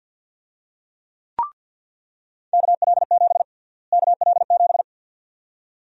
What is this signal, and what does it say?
Morse code, heard as a keyed tone, about 50 wpm, 700 Hz. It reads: KL7 KL7
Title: USA Call Sign Prefixes - Repeated 2x - 50wpm